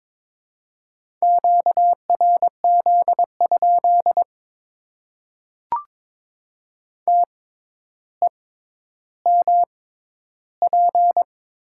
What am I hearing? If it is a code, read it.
Morse code, heard as a keyed tone, about 22 wpm, spaced as for 11 wpm, 700 Hz. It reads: QRZ? TEMP
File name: Top 90 QSO Elements - 6x Character Spacing - 22wpm